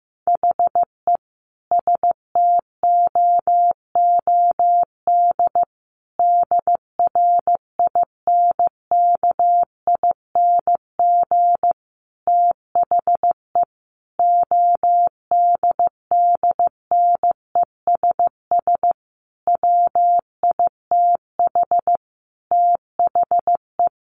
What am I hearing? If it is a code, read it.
Morse code, heard as a keyed tone, about 15 wpm, 700 Hz. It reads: HE STOOD DRINKING THE ODDNESS WITH THE
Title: A Sound Of Thunder 15wpm